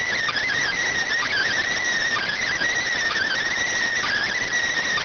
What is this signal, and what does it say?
Emergency Alert System (SAME) tones, heard as harsh buzzing bursts. It is heard from 0.3 to 4.5 s.